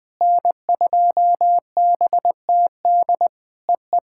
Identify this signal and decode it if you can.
Morse code, heard as a keyed tone, about 20 wpm, 700 Hz. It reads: N2BTD EE
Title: POTA QSO Practice - 20wpm